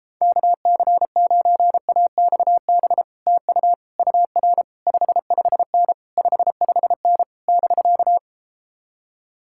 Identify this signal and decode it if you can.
Morse code, heard as a keyed tone, about 33 wpm, 700 Hz. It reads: KC9AXB TU UR 55N 55N <BK>